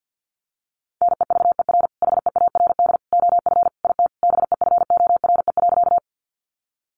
Morse code, heard as a keyed tone, about 50 wpm, 700 Hz. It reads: NEVER HEARD OF IT BEFORE.
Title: Sentences from Top 400 Words 50wpm